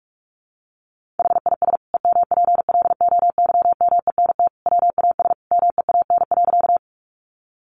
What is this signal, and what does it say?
Morse code, heard as a keyed tone, about 44 wpm, 700 Hz. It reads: HIS EMPLOYMENT WAS MEAN.
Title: Sentences from Top 1500 Words 45wpm